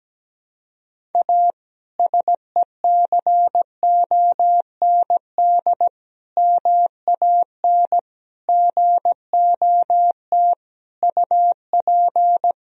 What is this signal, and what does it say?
Morse code, heard as a keyed tone, about 17 wpm, 700 Hz. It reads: A SECOND MAN GOT UP
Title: Sentences from Top 300 Words and Abbreviations 17wpm